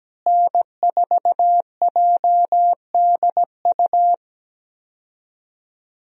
Morse code, heard as a keyed tone, about 17 wpm, 700 Hz. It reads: N4JDU
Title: USA Call Signs - 5 Characters - Repeated 2x - 17wpm